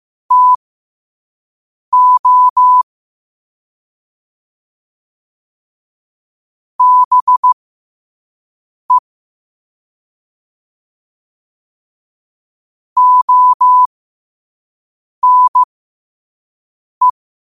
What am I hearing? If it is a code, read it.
Morse code, heard as a keyed tone, about 15 wpm, spaced as for 5 wpm, 1000 Hz. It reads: TO BE ONE
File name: qotdfiles / 5wpm